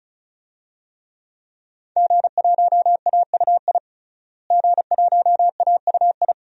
Morse code, heard as a keyed tone, about 36 wpm, 700 Hz. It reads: G1AUI G1AUI
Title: DXCC-100 Call Signs - 5 Characters - Repeated 2x - 35wpm